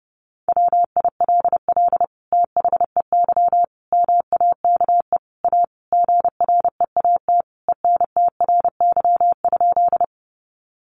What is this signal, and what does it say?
Morse code, heard as a keyed tone, about 30 wpm, 700 Hz. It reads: WILL THEY MAKE A GREAT ENTRY?
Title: Sentences from Top 1500 Words 30wpm